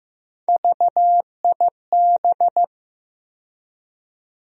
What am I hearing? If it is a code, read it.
Morse code, heard as a keyed tone, about 15 wpm, 700 Hz. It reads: VIB
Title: USA Call Sign Postfixes - 3 Letters 15wpm